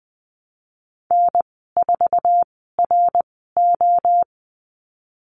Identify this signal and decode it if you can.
Morse code, heard as a keyed tone, about 20 wpm, spaced as for 15 wpm, 700 Hz. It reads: N4RO
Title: 069 Lesson 12g - Letter d - Call Signs 20wpm